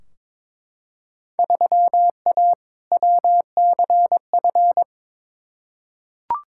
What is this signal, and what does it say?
Morse code, heard as a keyed tone, about 21 wpm, 700 Hz. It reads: 3A WCF
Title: Field Day Exchanges Encore - 22wpm